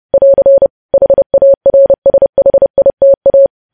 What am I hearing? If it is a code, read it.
Morse code, heard as a keyed tone, about 30 wpm, 550 Hz. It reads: <AR> HARSHITA
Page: news headlines in morse code at 30 wpm